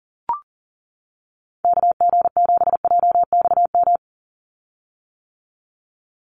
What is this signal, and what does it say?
Morse code, heard as a keyed tone, about 40 wpm, 700 Hz. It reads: KG7JXM
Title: USA Call Signs 40wpm